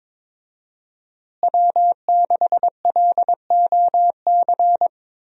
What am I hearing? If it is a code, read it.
Morse code, heard as a keyed tone, about 21 wpm, 700 Hz. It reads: W6LOC
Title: USA Call Signs - 5 Characters - Repeated 2x - 22wpm